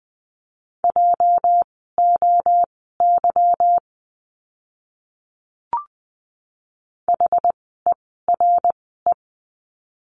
Morse code, heard as a keyed tone, about 20 wpm, spaced as for 15 wpm, 700 Hz. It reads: JOY HERE
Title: 206 Lesson 33f - Letter j - Words 20wpm